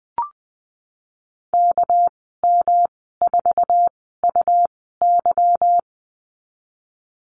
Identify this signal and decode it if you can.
Morse code, heard as a keyed tone, about 20 wpm, spaced as for 15 wpm, 700 Hz. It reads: KM4UY
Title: 200 Lesson 32g - Letter k - Call Signs 20wpm